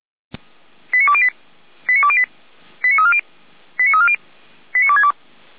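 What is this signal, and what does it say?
Five-tone selective call (ZVEI), heard as quick runs of five tones, tones of about 70 ms. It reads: 89208 89208 89340 89340 89262
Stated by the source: ZVEI five-tone selective call, code 89315, received with an SDR, from feuerwehr-orlamuende.org (fire brigade Orlamünde).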